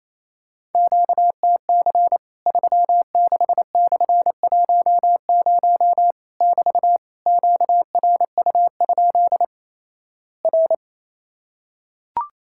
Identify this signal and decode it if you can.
Morse code, heard as a keyed tone, about 27 wpm, 700 Hz. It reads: QTC 36/10 <BT> QRU? R
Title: Worked All Europe DX Contest - QTC Exchange - 28wpm